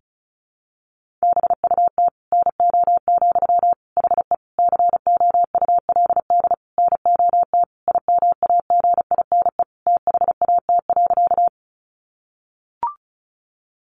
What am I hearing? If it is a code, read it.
Morse code, heard as a keyed tone, about 34 wpm, 700 Hz. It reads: BUT NO, HE COULD NOT IMAGINE THAT.
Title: Sentences from Top 1400 Words 35wpm